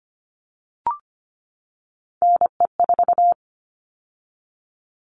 Morse code, heard as a keyed tone, about 25 wpm, 700 Hz. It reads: NE4
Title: All Cut Numbers - 3-Digit 25wpm